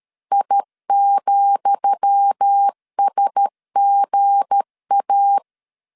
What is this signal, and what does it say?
Morse code, heard as a keyed tone, about 13 wpm, 800 Hz. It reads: I,SGA
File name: LEZIONI / 10 SIGNS / 10D COMA 12wpm 2544